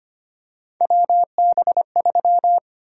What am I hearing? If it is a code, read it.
Morse code, heard as a keyed tone, about 25 wpm, 700 Hz. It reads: WB3
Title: USA Call Sign Prefixes - Repeated 2x - 25wpm